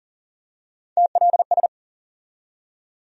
Morse code, heard as a keyed tone, about 40 wpm, 700 Hz. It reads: TLS
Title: USA Call Sign Postfixes - 3 Letters 40wpm